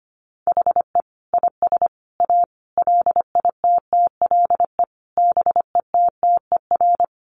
Morse code, heard as a keyed tone, about 25 wpm, 700 Hz. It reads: HE IS A LITTLE BETTER
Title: Sentences from Top 600 Words 25wpm